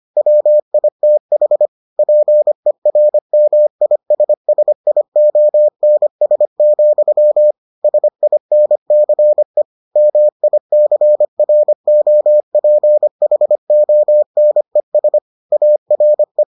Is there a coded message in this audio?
Yes: Morse code from 0.2 to 16.4 s, heard as a keyed tone, about 25 wpm, 600 Hz. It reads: WITH PERMISSIONS, SINCE MICROPHONES ARE